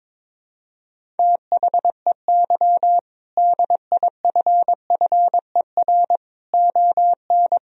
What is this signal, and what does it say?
Morse code, heard as a keyed tone, about 22 wpm, 700 Hz. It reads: THEY DIFFER ON
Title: Sentences from Top 1000 Words - Review Part 1 - 22wpm